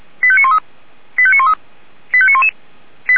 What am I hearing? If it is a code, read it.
Five-tone selective call (ZVEI), heard as quick runs of five tones, tones of about 70 ms. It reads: 85913 85913 85911
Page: SDR/ZVEI